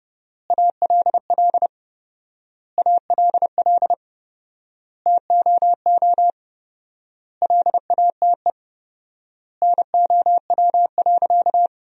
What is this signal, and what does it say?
Morse code, heard as a keyed tone, about 31 wpm, 700 Hz. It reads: ALL ALL TOO LATE NOW.
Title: 4x Word-Spacing - Sentences from Top 300 Words - 30wpm